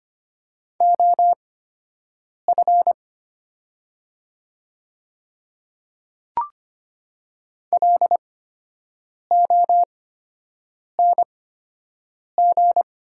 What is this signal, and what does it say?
Morse code, heard as a keyed tone, about 25 wpm, spaced as for 7 wpm, 700 Hz. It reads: OF LONG
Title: 8x Character Spacing - Top 200 Words - 25wpm